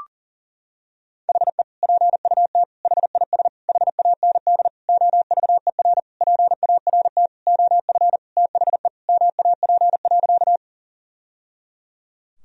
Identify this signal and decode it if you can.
Morse code, heard as a keyed tone, about 40 wpm, 700 Hz. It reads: HE PUT HIS HAND OVER PART OF THE MAP.
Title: Sentences from Top 1000 Words - Review Part 2 - 40wpm